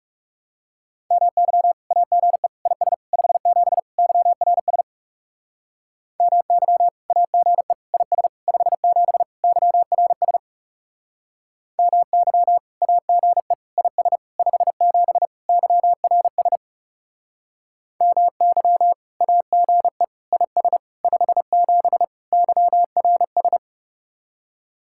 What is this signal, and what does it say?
Morse code, heard as a keyed tone, about 38 wpm, 700 Hz. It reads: MY AGE IS 57 YRS MY AGE IS 57 YRS MY AGE IS 57 YRS MY AGE IS 57 YRS